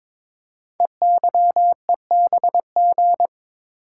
Morse code, heard as a keyed tone, about 23 wpm, 700 Hz. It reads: EYEBG
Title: Code Groups - 5 Letters - 22wpm